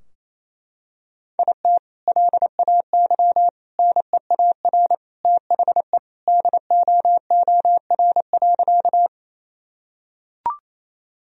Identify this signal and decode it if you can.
Morse code, heard as a keyed tone, about 28 wpm, 700 Hz. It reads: IT LAY NEAR THE DOOR.
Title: Sentences from Top 1000 Words - Review Part 2 - 28wpm